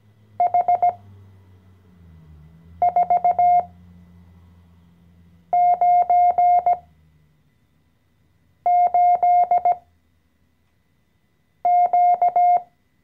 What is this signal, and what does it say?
Morse code, heard as a keyed tone, about 17 wpm, spaced as for 4 wpm, 710 Hz. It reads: H498Q
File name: CW-Online-Kurs / Anfänger / Lektion 014